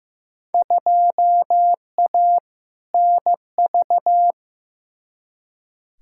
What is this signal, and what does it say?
Morse code, heard as a keyed tone, about 15 wpm, 700 Hz. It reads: 2A NV
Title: Field Day Exchanges - 15wpm